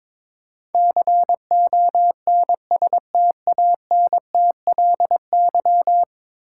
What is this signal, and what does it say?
Morse code, heard as a keyed tone, about 22 wpm, 700 Hz. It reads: CONSTANTLY
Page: morse-code-ninja-practice-22wpm